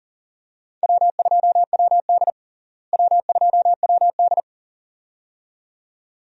Morse code, heard as a keyed tone, about 40 wpm, 700 Hz. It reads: W2WD W2WD
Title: USA Call Signs - 4 Characters - Repeated 2x - 40wpm